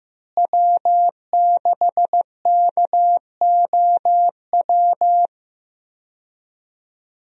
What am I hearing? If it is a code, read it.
Morse code, heard as a keyed tone, about 15 wpm, 700 Hz. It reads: W6KOW
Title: USA Call Signs - 5 Characters - Repeated 2x - 15wpm